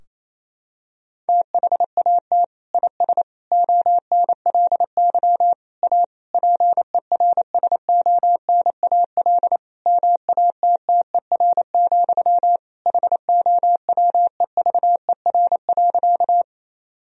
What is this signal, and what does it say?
Morse code, heard as a keyed tone, about 28 wpm, 700 Hz. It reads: THAT IS ONLY A PERSONAL MATTER, HOWEVER.
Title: Sentences from Top 1000 Words - Review Part 4 - 28wpm